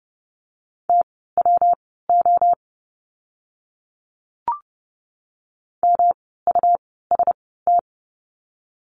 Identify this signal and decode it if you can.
Morse code, heard as a keyed tone, about 30 wpm, spaced as for 17 wpm, 700 Hz. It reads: TWO MUST
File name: course / 30:10wpm / 237 Lesson 38f - Letter q - Words 30wpm